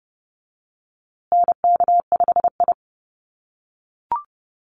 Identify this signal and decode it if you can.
Morse code, heard as a keyed tone, about 30 wpm, 700 Hz. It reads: NK5I
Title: USA Call Signs Repeated 2x 30wpm